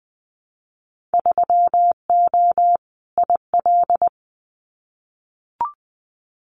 Morse code, heard as a keyed tone, about 20 wpm, 700 Hz. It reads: 3O IL
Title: Winter Field Day Exchanges 20wpm